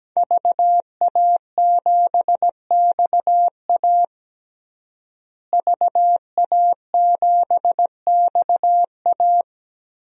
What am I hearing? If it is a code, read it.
Morse code, heard as a keyed tone, about 17 wpm, 700 Hz. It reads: VA7XA VA7XA
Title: DXCC-100 Call Signs - 5 Characters - Repeated 3x - 17wpm